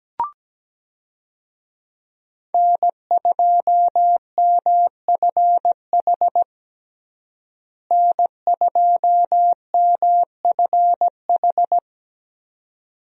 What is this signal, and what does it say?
Morse code, heard as a keyed tone, about 17 wpm, 700 Hz. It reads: N2MFH N2MFH
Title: USA Call Signs - 5 Characters - Repeated 2x - 17wpm